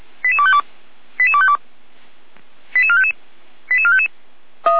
Five-tone selective call (ZVEI), heard as quick runs of five tones, tones of about 70 ms. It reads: 88252 88252 88455 88455